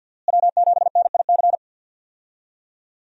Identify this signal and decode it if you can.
Morse code, heard as a keyed tone, about 52 wpm, 700 Hz. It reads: W7NIC
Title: USA Call Signs - 5 Characters - Repeated 3x - 50wpm